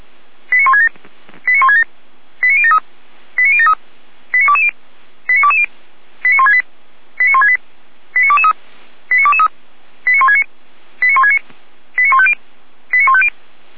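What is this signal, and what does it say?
Five-tone selective call (ZVEI), heard as quick runs of five tones, tones of about 80 ms. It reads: 89167 89167 89073 89073 89220 89220 89168 89168 89223 89223 89169 89169 89150 89150
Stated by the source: Five-tone ZVEI selective call received by SDR, from the Orlamünde fire brigade (feuerwehr-orlamuende) audio archive.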